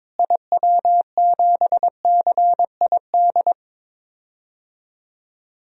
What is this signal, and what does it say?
Morse code, heard as a keyed tone, about 22 wpm, 700 Hz. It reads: IW7CID
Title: DXCC-100 Call Signs - 6 Characters - Repeated 2x - 22wpm